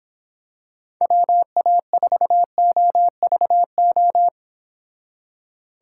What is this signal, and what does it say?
Morse code, heard as a keyed tone, about 26 wpm, 700 Hz. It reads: WA4OVO